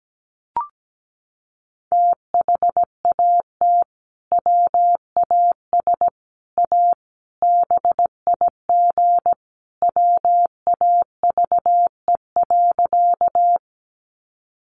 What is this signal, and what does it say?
Morse code, heard as a keyed tone, about 17 wpm, 700 Hz. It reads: THAT WAS A BIG WAVE.
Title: Sentences from Top 1400 Words 17wpm